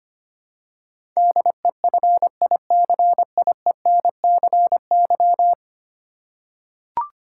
Morse code, heard as a keyed tone, about 25 wpm, 700 Hz. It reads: DEFICIENCY
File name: 10 Letter Words - First 3 Letters Given - 25wpm